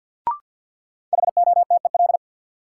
Morse code, heard as a keyed tone, about 50 wpm, 700 Hz. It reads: HOTEL
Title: Sentences from Top 1600 Words - New Word Drill - 50wpm